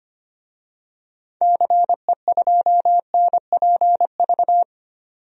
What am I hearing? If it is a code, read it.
Morse code, heard as a keyed tone, about 25 wpm, 700 Hz. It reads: CE2NPV
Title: DXCC-100 Call Signs - 6 Characters - 25wpm